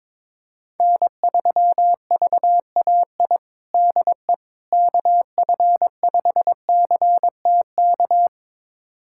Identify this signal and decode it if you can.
Morse code, heard as a keyed tone, about 22 wpm, 700 Hz. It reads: N3VAI DE KF5CTK
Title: USA Callsign DE Callsign 22wpm